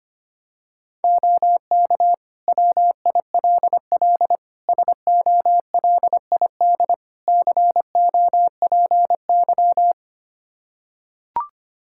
Morse code, heard as a keyed tone, about 25 wpm, 700 Hz. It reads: OK WILL SOLID COPY
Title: Practice QSO - Exchange 3 - More Info - 25wpm